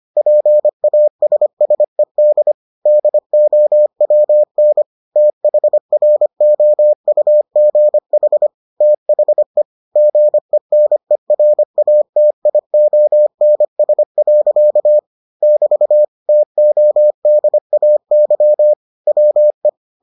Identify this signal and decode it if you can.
Morse code, heard as a keyed tone, about 25 wpm, 600 Hz. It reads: PASSED DOWN THROUGH THE GENERATIONS. <BT> TODAY WE